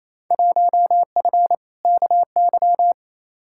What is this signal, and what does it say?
Morse code, heard as a keyed tone, about 28 wpm, 700 Hz. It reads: 1F KY
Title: Field Day Exchanges Encore - 28wpm